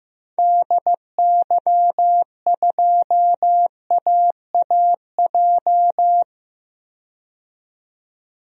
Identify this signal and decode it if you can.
Morse code, heard as a keyed tone, about 15 wpm, 700 Hz. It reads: DY2AAJ
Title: DXCC-100 Call Signs - 6 Characters - 15wpm